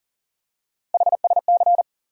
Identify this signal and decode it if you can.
Morse code, heard as a keyed tone, about 40 wpm, 700 Hz. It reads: HSC
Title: USA Call Sign Postfixes - 3 Letters 40wpm